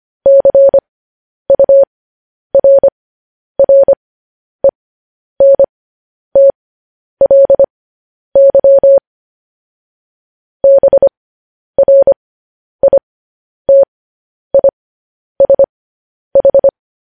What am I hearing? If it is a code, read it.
Morse code, heard as a keyed tone, about 25 wpm, spaced as for 10 wpm, 550 Hz. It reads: CURRENTLY BRITISH
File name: podcast / News headlines-1x-25x10